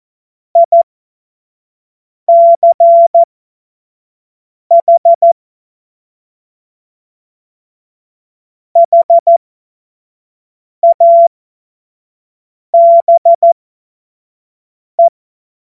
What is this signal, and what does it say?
Morse code, heard as a keyed tone, about 14 wpm, spaced as for 5 wpm, 680 Hz. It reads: ICH HABE
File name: Morsekurs / Witze / Witz 27. April 2021, 25 F70